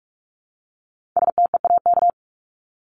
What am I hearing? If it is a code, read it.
Morse code, heard as a keyed tone, about 44 wpm, 700 Hz. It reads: STEAK
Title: Top 1251-1500 Words 45wpm